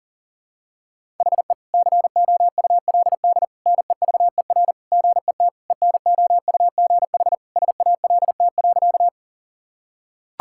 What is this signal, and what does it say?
Morse code, heard as a keyed tone, about 37 wpm, 700 Hz. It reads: HE COULD NEVER GET ENOUGH SALT.
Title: Sentences from Top 1000 Words - Review Part 5 - 40wpm